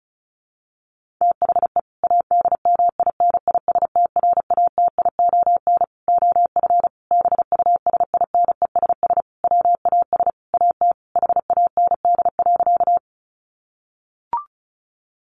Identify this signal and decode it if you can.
Morse code, heard as a keyed tone, about 34 wpm, 700 Hz. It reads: THE ADMINISTRATION OF BUSINESS WAS AT HAND.
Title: Sentences from Top 1300 Words 35wpm